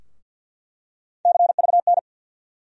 Morse code, heard as a keyed tone, about 48 wpm, 700 Hz. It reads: CVN
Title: USA Call Sign Postfixes - 3 Letters 50wpm